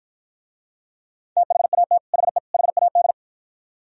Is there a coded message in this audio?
Yes: Morse code from 1.4 to 3.1 s, heard as a keyed tone, about 52 wpm, 700 Hz. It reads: THAT HE HAD